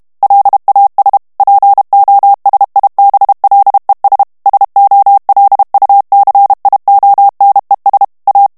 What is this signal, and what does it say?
Morse code, heard as a keyed tone, about 32 wpm, 800 Hz. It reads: LAS POSIBLES SOLUCIONES A
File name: morse mp3 / ESPAGNOL TEXTE EN CLAIR 1 VITESSE 30 MOTS MINUTES